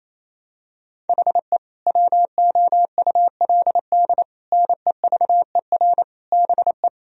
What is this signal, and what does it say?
Morse code, heard as a keyed tone, about 28 wpm, 700 Hz. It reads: HE WOULD NEVER BE
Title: Sentences from Top 1000 Words - Review Part 3 - 28wpm